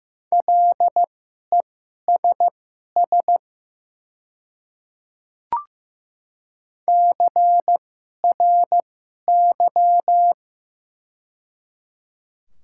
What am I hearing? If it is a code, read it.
Morse code, heard as a keyed tone, about 15 wpm, spaced as for 11 wpm, 700 Hz. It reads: LESS CRY
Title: 2x Character Spacing - Top 500 Words - 15wpm